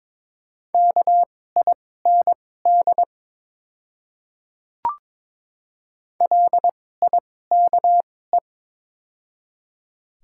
Morse code, heard as a keyed tone, about 22 wpm, spaced as for 16 wpm, 700 Hz. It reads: KIND LIKE
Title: 2x Character Spacing - Top 300 Words - 22wpm